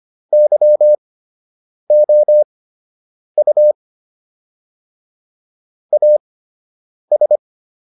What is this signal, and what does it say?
Morse code, heard as a keyed tone, about 25 wpm, spaced as for 8 wpm, 600 Hz. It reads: YOU AS